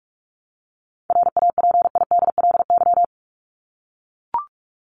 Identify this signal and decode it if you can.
Morse code, heard as a keyed tone, about 44 wpm, 700 Hz. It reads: RAPIDLY
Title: LY Suffix - Top 100 Words with Base Words 45wpm